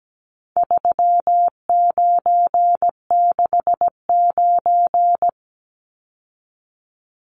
Morse code, heard as a keyed tone, about 17 wpm, 700 Hz. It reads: 3969
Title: Numbers - 4-Digit 17wpm